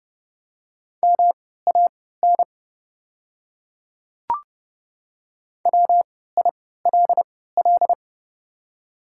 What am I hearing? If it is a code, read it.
Morse code, heard as a keyed tone, about 31 wpm, spaced as for 17 wpm, 700 Hz. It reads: MAN WILL
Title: 3x Character Spacing - Top 300 Words - 30wpm